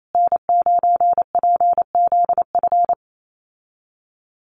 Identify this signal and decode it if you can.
Morse code, heard as a keyed tone, about 28 wpm, 700 Hz. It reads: N9PZF
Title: USA Call Signs Repeated 3x 28wpm